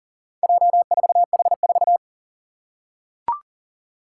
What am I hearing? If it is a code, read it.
Morse code, heard as a keyed tone, about 40 wpm, 700 Hz. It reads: J4H4
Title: Four Challenging Letter-Number Combinations 40wpm